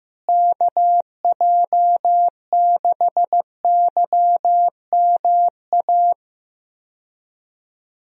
Morse code, heard as a keyed tone, about 15 wpm, 700 Hz. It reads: KJ6YMA
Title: USA Call Signs - 6 Characters - Repeated 3x - 15wpm